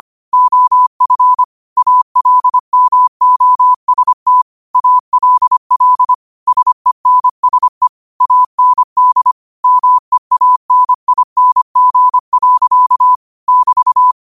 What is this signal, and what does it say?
Morse code, heard as a keyed tone, about 25 wpm, 1000 Hz. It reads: OF ALMOST ALL SENSE AND MEANING. <BT>
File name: qotdfiles / 25wpm